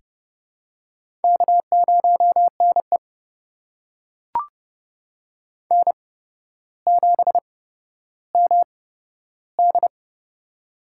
Morse code, heard as a keyed tone, about 31 wpm, spaced as for 13 wpm, 700 Hz. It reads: K0NE N7MD
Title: USA Call Signs - 4 Characters - 8x Character Spacing - 30wpm